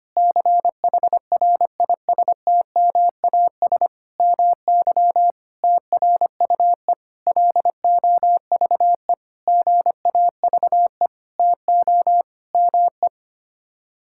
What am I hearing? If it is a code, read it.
Morse code, heard as a keyed tone, about 25 wpm, 700 Hz. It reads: CHRISTMAS MY TRUE LOVE GAVE TO ME
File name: Sets of 9 Words - 25wpm